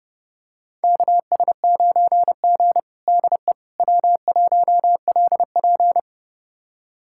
Morse code, heard as a keyed tone, about 30 wpm, 700 Hz. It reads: KS9G DE W1LP
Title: USA Callsign DE Callsign - 4 Characters - 30wpm